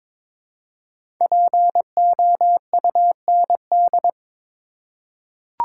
Morse code, heard as a keyed tone, about 22 wpm, 700 Hz. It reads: POUND
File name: 5 Letter Words - First 3 Letters Given - 22wpm